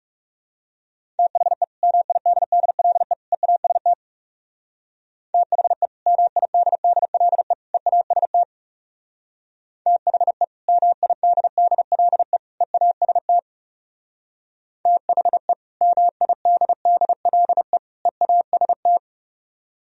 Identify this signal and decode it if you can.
Morse code, heard as a keyed tone, about 37 wpm, 700 Hz. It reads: THE MIDDLE EAST THE MIDDLE EAST THE MIDDLE EAST THE MIDDLE EAST
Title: Speed-Racing - Sets of 3 Words 30wpm